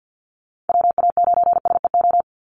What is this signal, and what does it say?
Morse code, heard as a keyed tone, about 50 wpm, 700 Hz. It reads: WA9SEO
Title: Call Signs - 50wpm